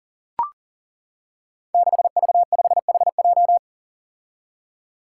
Morse code, heard as a keyed tone, about 40 wpm, 700 Hz. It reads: 6V5HJ
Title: Five Challenging Letter-Number Combinations 40wpm